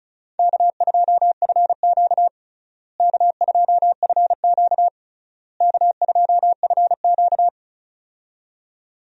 Morse code, heard as a keyed tone, about 34 wpm, 700 Hz. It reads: K2FQ K2FQ K2FQ
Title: USA Call Signs - 4 Characters - Repeated 3x - 35wpm